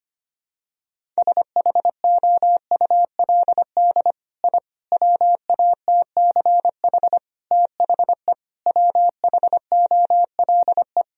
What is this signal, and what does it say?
Morse code, heard as a keyed tone, about 25 wpm, 700 Hz. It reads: SHOULD I WATCH THE WHOLE
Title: Sentences from Top 1000 Words - Review Part 4 - 25wpm